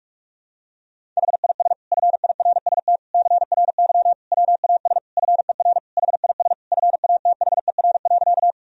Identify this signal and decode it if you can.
Morse code, heard as a keyed tone, about 46 wpm, 700 Hz. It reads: HIS FIRST CRY WAS FER HIS FATHER.
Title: Sentences from Top 500 Words and Abbreviations 45wpm